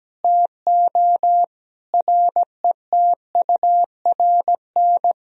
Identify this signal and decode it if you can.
Morse code, heard as a keyed tone, about 17 wpm, 700 Hz. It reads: TO RETURN
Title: Sentences from Top 1000 Words - Review Part 1 - 17wpm